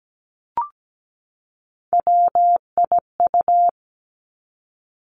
Morse code, heard as a keyed tone, about 17 wpm, 700 Hz. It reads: WIU
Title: Three Challenging Letter-Number Combinations 17wpm